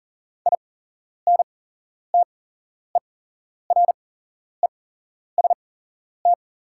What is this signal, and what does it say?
Morse code, heard as a keyed tone, about 40 wpm, spaced as for 11 wpm, 690 Hz. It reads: INTEREST
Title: 8x Character Spacing - Top 500 Words - 40wpm